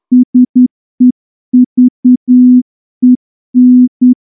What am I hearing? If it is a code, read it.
Morse code, heard as a keyed tone, about 9 wpm, 250 Hz. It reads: SEVEN
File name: morse-training-10WPM